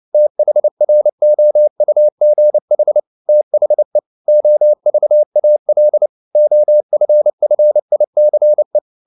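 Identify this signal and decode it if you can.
Morse code, heard as a keyed tone, about 29 wpm, 600 Hz. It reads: THROUGH THE OVAL OFFICE
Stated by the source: Morse code at 29 wpm, sending Ken Follett's novel Never in English, from CW med Gnister.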